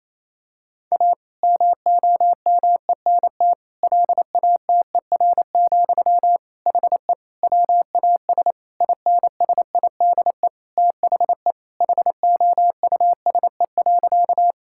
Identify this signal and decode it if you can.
Morse code, heard as a keyed tone, about 27 wpm, 700 Hz. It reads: A MOMENT LATER, HE WAS INSIDE THE HOUSE.
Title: Sentences from Top 1000 Words - Review Part 6 - 28wpm